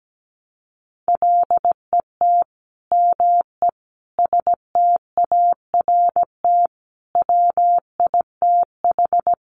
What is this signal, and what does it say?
Morse code, heard as a keyed tone, about 17 wpm, 700 Hz. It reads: LET ME START WITH
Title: Sets of 5 Words Encore 17wpm